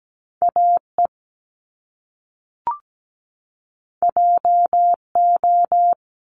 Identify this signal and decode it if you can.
Morse code, heard as a keyed tone, about 17 wpm, 700 Hz. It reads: AE JO
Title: Top Two Letter Combinations in Words 17wpm